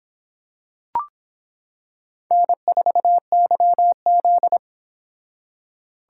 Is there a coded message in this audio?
Yes: Morse code from 2.3 to 4.6 s, heard as a keyed tone, about 26 wpm, 700 Hz. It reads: N4YZ